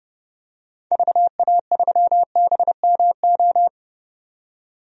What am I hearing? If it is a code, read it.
Morse code, heard as a keyed tone, about 30 wpm, 700 Hz. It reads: VA3BMO
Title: DXCC-100 Call Signs - 6 Characters - 30wpm